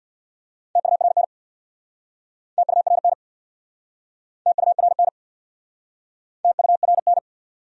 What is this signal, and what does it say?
Morse code, heard as a keyed tone, about 69 wpm, 700 Hz. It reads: TURN TURN TURN TURN